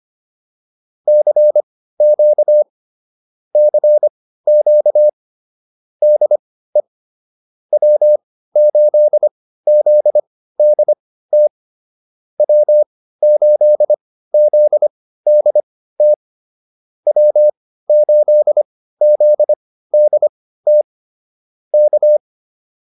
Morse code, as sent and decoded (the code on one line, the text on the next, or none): -.-. --.- / -.-. --.- / -.. / .-- ---.. --.. -.. - / .-- ---.. --.. -.. - / .-- ---.. --.. -.. - / -.-
CQ CQ DE W8ZDT W8ZDT W8ZDT K